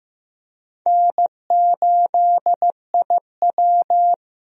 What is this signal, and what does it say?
Morse code, heard as a keyed tone, about 15 wpm, 700 Hz. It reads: N8IW
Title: USA Call Signs - 4 Characters - 15wpm